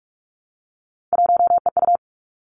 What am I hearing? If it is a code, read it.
Morse code, heard as a keyed tone, about 46 wpm, 700 Hz. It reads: 1EU
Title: Three Challenging Letter-Number Combinations 45wpm